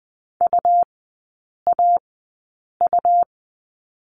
Morse code, heard as a keyed tone, about 20 wpm, spaced as for 8 wpm, 700 Hz. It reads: UAU
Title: 100 Lesson 17e - Letter u - Three Characters 20wpm